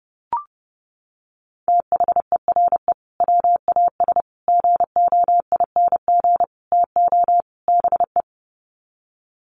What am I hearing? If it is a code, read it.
Morse code, heard as a keyed tone, about 30 wpm, 700 Hz. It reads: THERE WAS GOING TO BE
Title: Sets of 5 Words Encore2 30wpm